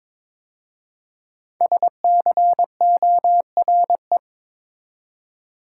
Morse code, heard as a keyed tone, about 23 wpm, 700 Hz. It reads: SCORE
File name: 5 Letter Words - First 3 Letters Given - 22wpm